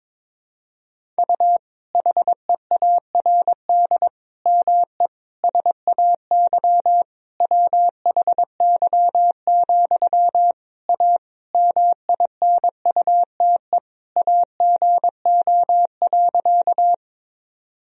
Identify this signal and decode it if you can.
Morse code, heard as a keyed tone, about 22 wpm, 700 Hz. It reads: U HEARD ME SAY WHY, A MINUTE AGO.
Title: Sentences from Top 500 Words and Abbreviations 22wpm